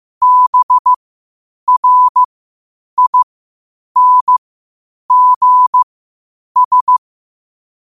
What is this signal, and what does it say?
Morse code, heard as a keyed tone, about 15 wpm, spaced as for 8 wpm, 1000 Hz. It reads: BRINGS